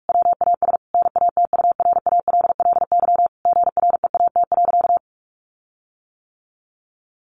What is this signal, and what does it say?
Morse code, heard as a keyed tone, about 44 wpm, 700 Hz. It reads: WAS NATURALLY GREAT.
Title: Sentences from Top 1300 Words 45wpm